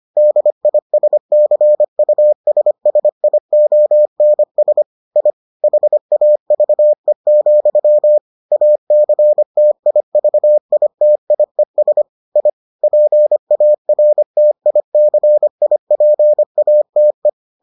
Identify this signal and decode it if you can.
Morse code, heard as a keyed tone, about 25 wpm, 600 Hz. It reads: DISCUSSIONS I HAVE, ACTIVITIES I PARTICIPATE